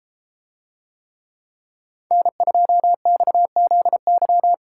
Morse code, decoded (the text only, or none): N2XZY